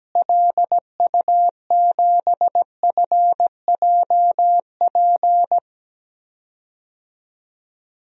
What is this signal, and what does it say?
Morse code, heard as a keyed tone, about 17 wpm, 700 Hz. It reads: LU7FJP
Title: DXCC-100 Call Signs - 6 Characters - Repeated 3x - 17wpm